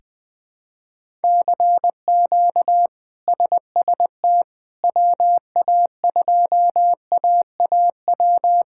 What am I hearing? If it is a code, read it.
Morse code, heard as a keyed tone, about 20 wpm, 700 Hz. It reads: CQ SST WA2AAW
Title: K1USN SST QSO Practice - 20wpm